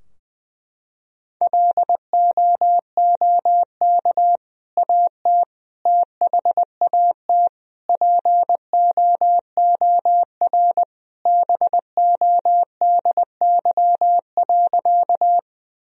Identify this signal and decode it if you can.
Morse code, heard as a keyed tone, about 21 wpm, 700 Hz. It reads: LOOK AT THAT POOR BODY.
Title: Sentences from Top 1000 Words - Review Part 2 - 20wpm